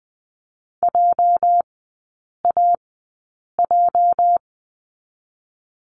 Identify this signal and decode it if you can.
Morse code, heard as a keyed tone, about 20 wpm, spaced as for 8 wpm, 690 Hz. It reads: JAJ
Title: 205 Lesson 33e - Letter j - Three Characters 20wpm